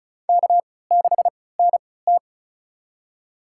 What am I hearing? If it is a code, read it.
Morse code, heard as a keyed tone, about 36 wpm, spaced as for 20 wpm, 700 Hz. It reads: K6NT
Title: USA Call Signs - 4 Characters - 3x Character Spacing - 35wpm